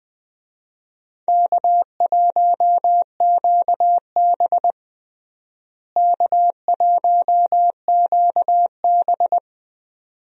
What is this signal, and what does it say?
Morse code, heard as a keyed tone, about 20 wpm, 700 Hz. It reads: K1QB K1QB
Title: USA Call Signs - 4 Characters - Repeated 2x - 20wpm